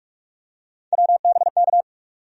Morse code, heard as a keyed tone, about 43 wpm, 700 Hz. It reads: WBK
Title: USA Call Sign Postfixes - 3 Letters 45wpm